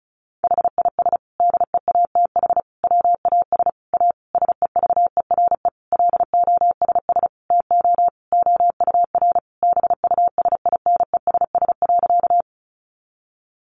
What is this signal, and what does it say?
Morse code, heard as a keyed tone, about 34 wpm, 700 Hz. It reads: HIS DEATH WAS A SEVERE LOSS TO OUR BUSINESS.
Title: Sentences from Top 1400 Words 35wpm